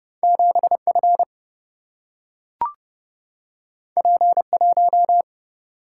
Morse code, heard as a keyed tone, about 30 wpm, 700 Hz. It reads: 7F P1
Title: Single-Letter Single-Number 30wpm